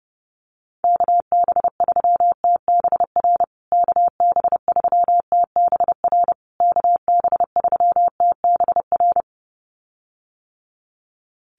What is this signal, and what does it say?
Morse code, heard as a keyed tone, about 30 wpm, 700 Hz. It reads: KB3TBR KB3TBR KB3TBR
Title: USA Call Signs Repeated 3x 30wpm